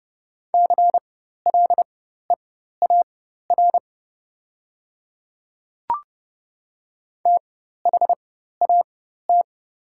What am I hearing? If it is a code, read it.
Morse code, heard as a keyed tone, about 30 wpm, spaced as for 14 wpm, 690 Hz. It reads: CLEAR THAT
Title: 4x Character Spacing - Top 500 Words - 30wpm